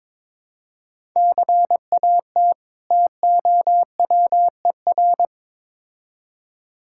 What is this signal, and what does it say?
Morse code, heard as a keyed tone, about 22 wpm, 700 Hz. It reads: CAT TOWER
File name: Difficult Words with Hints - 22wpm